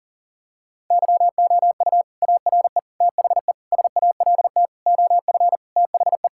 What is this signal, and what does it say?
Morse code, heard as a keyed tone, about 40 wpm, 700 Hz. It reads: YOU ARE THE SALT OF THE
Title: Sentences from Top 1000 Words - Review Part 5 - 40wpm